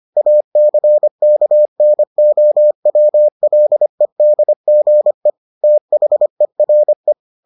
Morse code, heard as a keyed tone, about 25 wpm, 600 Hz. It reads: ACKNOWLEDGE THERE